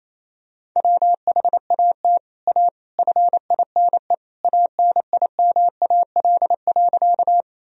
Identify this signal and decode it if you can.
Morse code, heard as a keyed tone, about 28 wpm, 700 Hz. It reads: WHAT A FINE ANIMAL.
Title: Sentences from Top 1000 Words - Review Part 1 - 28wpm